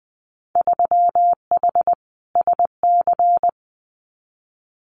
Morse code, heard as a keyed tone, about 20 wpm, 700 Hz. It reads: 3H SC
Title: Winter Field Day Exchanges 20wpm